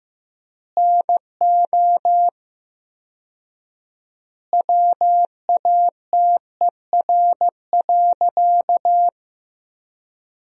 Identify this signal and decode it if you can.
Morse code, heard as a keyed tone, about 15 wpm, 700 Hz. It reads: NO WATER.
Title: 4x Word-Spacing - Sentences from Top 100 Words - 15wpm